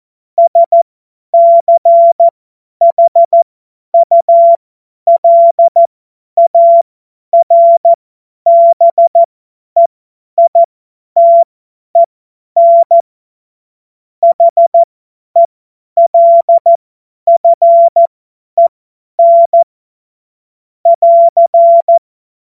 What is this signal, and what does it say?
Morse code, heard as a keyed tone, about 14 wpm, spaced as for 10 wpm, 680 Hz. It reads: SCHULARBEITEN HELFEN <AR>
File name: Morsekurs / Witze / Witz 27. September 2022 50 F70